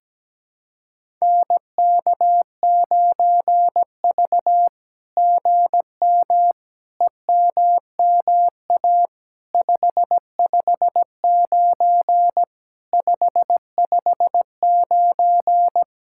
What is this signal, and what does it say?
Morse code, heard as a keyed tone, about 17 wpm, 700 Hz. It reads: NK9V GM EMMA 559 559
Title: SOTA QSO Practice 17wpm